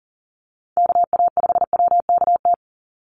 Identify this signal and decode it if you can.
Morse code, heard as a keyed tone, about 40 wpm, 700 Hz. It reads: KA5WKT
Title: USA Call Signs 40wpm